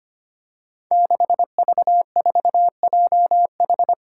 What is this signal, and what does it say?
Morse code, heard as a keyed tone, about 25 wpm, 700 Hz. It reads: 6V4JH
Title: Five Challenging Letter-Number Combinations 25wpm